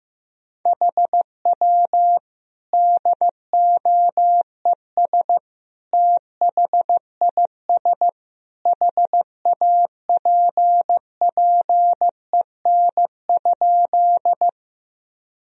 Morse code, heard as a keyed tone, about 15 wpm, 700 Hz. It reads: HW DOES THIS HAPPEN?
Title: Sentences from Top 400 Words and Abbreviations 15wpm